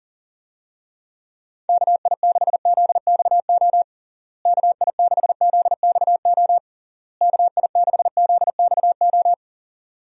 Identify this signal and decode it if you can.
Morse code, heard as a keyed tone, about 40 wpm, 700 Hz. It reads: KI6ZXO KI6ZXO KI6ZXO
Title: USA Call Signs - 6 Characters - Repeated 3x - 40wpm